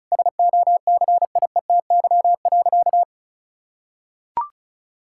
Morse code, heard as a keyed tone, about 36 wpm, 700 Hz. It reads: SOCIETY.
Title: Sentences from Top 1600 Words 35wpm